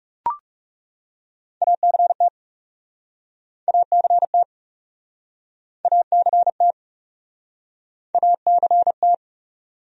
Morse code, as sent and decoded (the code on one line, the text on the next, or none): .- -.-. - / .- -.-. - / .- -.-. - / .- -.-. -
ACT ACT ACT ACT